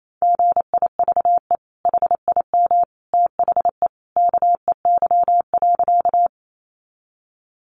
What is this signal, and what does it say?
Morse code, heard as a keyed tone, about 28 wpm, 700 Hz. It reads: GIVE HIM THE KEY.
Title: Sentences from Top 600 Words 28wpm